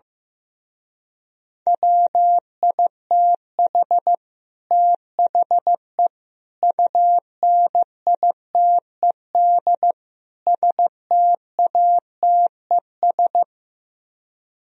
Morse code, heard as a keyed tone, about 15 wpm, 700 Hz. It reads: WITH THE UNITED STATES
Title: Speed-Racing - Sets of 4 Words 15wpm